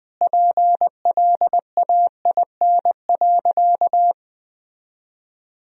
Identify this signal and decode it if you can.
Morse code, heard as a keyed tone, about 20 wpm, 700 Hz. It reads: PLAIN.
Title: 4x Word-Spacing - Sentences from Top 400 Words - 20wpm